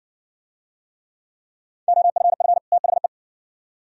Morse code, heard as a keyed tone, about 60 wpm, 700 Hz. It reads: OFF THE